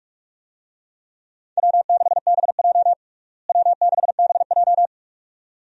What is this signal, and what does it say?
Morse code, heard as a keyed tone, about 47 wpm, 700 Hz. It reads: W6BJ W6BJ
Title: USA Call Signs - 4 Characters - Repeated 2x - 45wpm